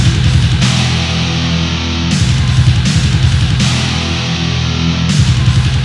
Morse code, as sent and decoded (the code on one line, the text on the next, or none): none